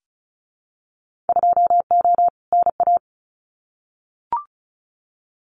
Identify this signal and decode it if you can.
Morse code, heard as a keyed tone, about 36 wpm, 700 Hz. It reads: 2O NA
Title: Winter Field Day Exchanges 35wpm